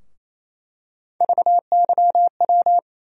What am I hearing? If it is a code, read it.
Morse code, heard as a keyed tone, about 28 wpm, 700 Hz. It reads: VYW
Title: USA Call Sign Postfixes - 3 Letters 28wpm